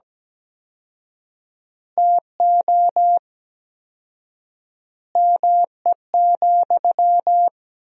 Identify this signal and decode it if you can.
Morse code, heard as a keyed tone, about 17 wpm, 700 Hz. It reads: TO ME,
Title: 4x Word-Spacing - Sentences from Top 500 Words - 17wpm